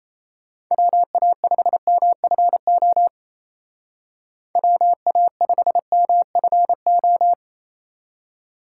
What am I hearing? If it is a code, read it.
Morse code, heard as a keyed tone, about 33 wpm, 700 Hz. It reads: WA5MFO WA5MFO
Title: Speed Racing - USA 2x3 Call Signs 28wpm